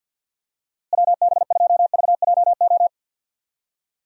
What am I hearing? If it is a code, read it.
Morse code, heard as a keyed tone, about 53 wpm, 700 Hz. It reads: WB2VJO
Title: USA Call Signs - 6 Characters - Repeated 2x - 50wpm